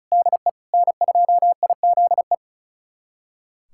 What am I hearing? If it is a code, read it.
Morse code, heard as a keyed tone, about 36 wpm, 700 Hz. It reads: DE N2IZE
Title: USA Callsign DE Callsign 35wpm